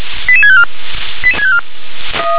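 Five-tone selective call (ZVEI), heard as quick runs of five tones, tones of about 70 ms. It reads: 88654
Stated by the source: ZVEI five-tone selective call received on SDR.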